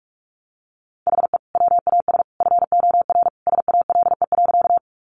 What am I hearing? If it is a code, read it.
Morse code, heard as a keyed tone, about 44 wpm, 700 Hz. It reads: HE WAS FOR SALE.
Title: Sentences from Top 1400 Words 45wpm